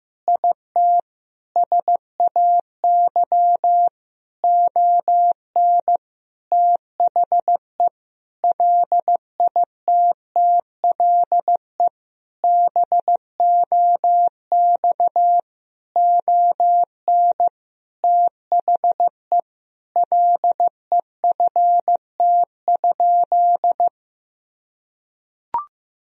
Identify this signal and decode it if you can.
Morse code, heard as a keyed tone, about 15 wpm, 700 Hz. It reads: IT SAY ON THE LITTLE BOX ON THE LEFT?
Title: Sentences from Top 1000 Words - Review Part 2 - 15wpm